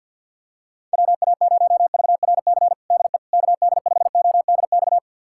Morse code, heard as a keyed tone, about 49 wpm, 700 Hz. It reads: WA0VRC DE KD5ODX
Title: USA Callsign DE Callsign - 6 Characters - 50wpm